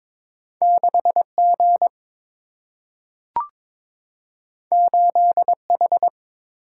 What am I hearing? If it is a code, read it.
Morse code, heard as a keyed tone, about 22 wpm, 700 Hz. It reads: 6 G 8 H
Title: Single-Letter Single-Number 22wpm